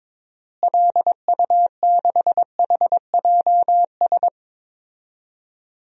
Morse code, heard as a keyed tone, about 22 wpm, 700 Hz. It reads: LU6HJS
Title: DXCC-100 Call Signs - 6 Characters - Repeated 3x - 22wpm